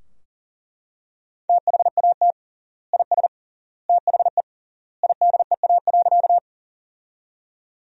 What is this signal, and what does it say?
Morse code, heard as a keyed tone, about 40 wpm, 700 Hz. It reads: THAT IS THE IDEA.
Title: 3x Word-Spacing - Sentences from Top 300 Words - 40wpm